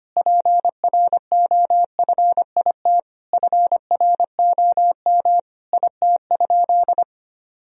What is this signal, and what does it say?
Morse code, heard as a keyed tone, about 25 wpm, 700 Hz. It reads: PROFIT FROM IT?
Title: Sentences from Top 1000 Words - Review Part 4 - 25wpm